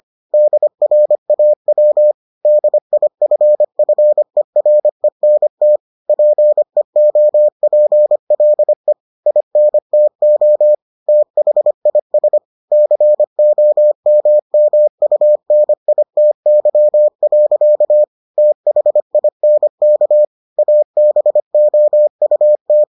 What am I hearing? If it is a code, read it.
Morse code, heard as a keyed tone, about 25 wpm, 600 Hz. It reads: DRAW DIFFERENT PEOPLE INTO THIS COMMUNITY. THINK ABOUT